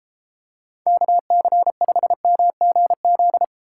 Morse code, heard as a keyed tone, about 33 wpm, 700 Hz. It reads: KC5MGZ